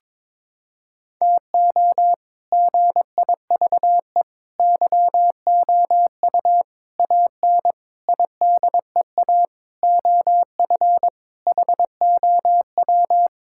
Morse code, heard as a keyed tone, about 22 wpm, 700 Hz. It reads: TO GIVE YOU AN IDEA OF HOW
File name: practice / 22wpm / Sets of 7 Words - 22wpm